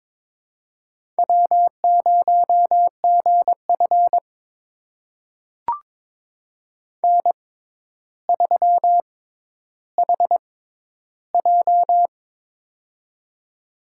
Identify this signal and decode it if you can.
Morse code, heard as a keyed tone, about 23 wpm, spaced as for 11 wpm, 700 Hz. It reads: W0GF N3HJ
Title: USA Call Signs - 4 Characters - 6x Character Spacing - 22wpm